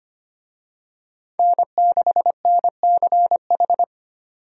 Morse code, heard as a keyed tone, about 25 wpm, 700 Hz. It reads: N6NCH
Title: USA Call Signs - 5 Characters - 25wpm